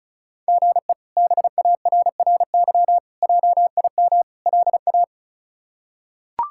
Morse code, heard as a keyed tone, about 36 wpm, 700 Hz. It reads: GE BARRY JIM LA
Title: K1USN SST QSO Practice - 35wpm